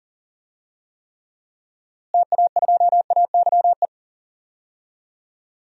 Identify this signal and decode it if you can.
Morse code, heard as a keyed tone, about 40 wpm, 700 Hz. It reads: TA2AYE
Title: DXCC-100 Call Signs - 6 Characters - Repeated 2x - 40wpm